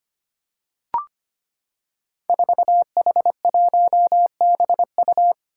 Morse code, heard as a keyed tone, about 25 wpm, 700 Hz. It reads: 4H1BU
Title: Five Challenging Letter-Number Combinations 25wpm